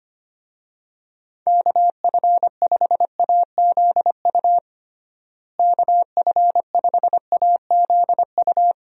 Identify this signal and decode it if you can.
Morse code, heard as a keyed tone, about 25 wpm, 700 Hz. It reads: KF5AZU KF5AZU
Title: USA Call Signs - 6 Characters - Repeated 2x - 25wpm